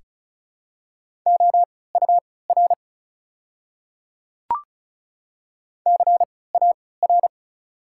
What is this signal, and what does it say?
Morse code, heard as a keyed tone, about 36 wpm, spaced as for 20 wpm, 690 Hz. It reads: OUR CAR
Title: 3x Character Spacing - Top 300 Words - 35wpm